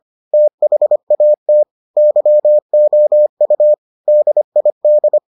Morse code, heard as a keyed tone, about 25 wpm, 600 Hz. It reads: THAT YOU DID